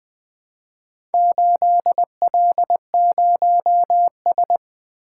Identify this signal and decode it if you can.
Morse code, heard as a keyed tone, about 20 wpm, 700 Hz. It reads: 8L0S
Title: Four Letter-Number Combinations - 20wpm